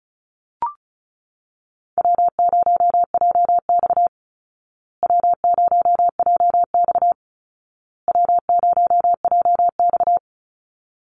Morse code, heard as a keyed tone, about 34 wpm, 700 Hz. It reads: W0JX W0JX W0JX
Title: Easy Call Signs Repeated 3x - 35wpm